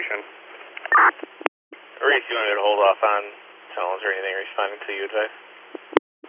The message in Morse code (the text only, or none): none